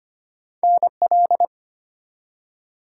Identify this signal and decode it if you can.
Morse code, heard as a keyed tone, about 25 wpm, 700 Hz. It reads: NL
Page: morse-code-ninja-practice-25wpm